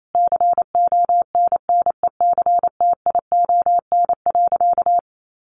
Morse code, heard as a keyed tone, about 28 wpm, 700 Hz. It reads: CONNECTION.
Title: Sentences from Top 1300 Words 28wpm